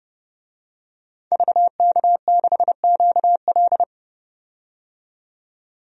Morse code, heard as a keyed tone, about 30 wpm, 700 Hz. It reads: VK6QL
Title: DXCC-100 Call Signs - 5 Characters - 30wpm